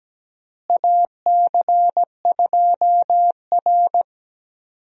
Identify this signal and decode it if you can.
Morse code, heard as a keyed tone, about 17 wpm, 700 Hz. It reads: AC2R
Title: USA Call Signs - 4 Characters - Repeated 2x - 17wpm